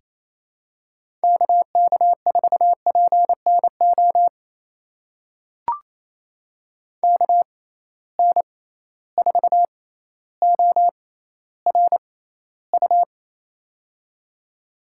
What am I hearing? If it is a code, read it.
Morse code, heard as a keyed tone, about 27 wpm, spaced as for 14 wpm, 700 Hz. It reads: KK4PNO KN4ORU
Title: USA Call Signs - 6 Characters - 6x Character Spacing - 28wpm